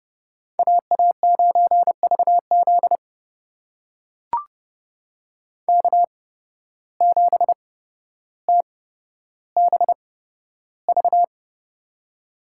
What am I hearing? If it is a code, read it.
Morse code, heard as a keyed tone, about 30 wpm, spaced as for 13 wpm, 700 Hz. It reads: AA9VZ K7TBV
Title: USA Call Signs - 5 Characters - 8x Character Spacing - 30wpm